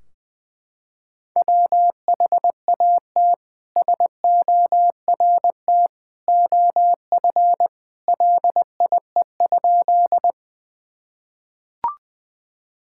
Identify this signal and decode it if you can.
Morse code, heard as a keyed tone, about 20 wpm, 700 Hz. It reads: WHAT SORT OF LIE?
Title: Sentences from Top 1600 Words 20wpm